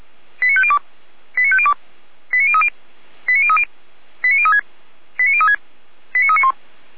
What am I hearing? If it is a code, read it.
Five-tone selective call (ZVEI), heard as quick runs of five tones, tones of about 70 ms. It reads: 89502 89502 89030 89030 89036 89036 89391